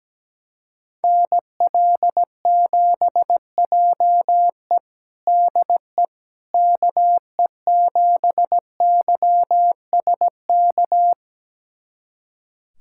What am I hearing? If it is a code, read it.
Morse code, heard as a keyed tone, about 17 wpm, 700 Hz. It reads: NL7JE DE KE7YSK
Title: USA Callsign DE Callsign 17wpm